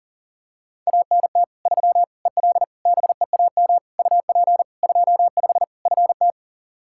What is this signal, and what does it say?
Morse code, heard as a keyed tone, about 40 wpm, 700 Hz. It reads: ANT 3 EL BEAM UP 25 FT
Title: Practice QSO - Exchange 2 - Info - 40wpm